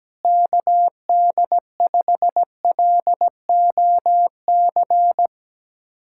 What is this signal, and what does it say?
Morse code, heard as a keyed tone, about 17 wpm, 700 Hz. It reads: KD5LOC
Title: USA Callsign DE Callsign 17wpm